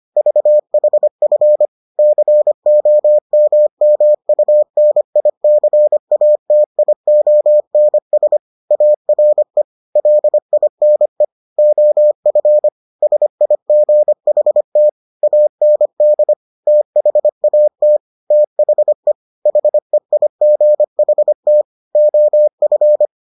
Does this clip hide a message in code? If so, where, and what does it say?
Yes: Morse code from 0.2 to 23.1 s, heard as a keyed tone, about 25 wpm, 600 Hz. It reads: VHF COMMUNICATIONS ARE LINE OF SIGHT AND THAT THE HEIGHT OF